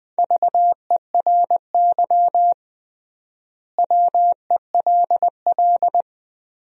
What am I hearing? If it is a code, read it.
Morse code, heard as a keyed tone, about 20 wpm, 700 Hz. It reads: VERY WELL
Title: 3x Word-Spacing - Sentences from Top 200 Words - 20wpm